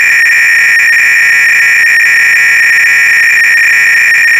300-baud modem data.
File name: Repository / ACR-Audio-Cassette-Programs / MITS-BASIC-GAME-BATTLE-CLOADC